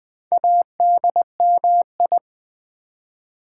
Morse code, heard as a keyed tone, about 20 wpm, 700 Hz. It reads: ADMI
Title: Build A Word - 2 Letters At A Time - 6-Letter Words - 20wpm